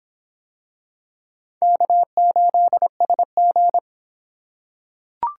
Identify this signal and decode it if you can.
Morse code, heard as a keyed tone, about 25 wpm, 700 Hz. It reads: K8SG